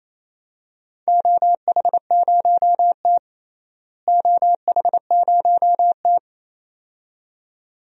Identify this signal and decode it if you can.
Morse code, heard as a keyed tone, about 28 wpm, 700 Hz. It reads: OH0T OH0T
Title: DXCC-100 Call Signs - 4 Characters - Repeated 2x - 28wpm